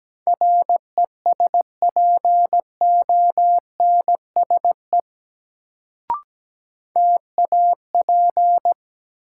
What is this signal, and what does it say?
Morse code, heard as a keyed tone, about 17 wpm, 700 Hz. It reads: RESPONSE TAP
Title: Sentences from Top 1400 Words - New Word Drill - 17wpm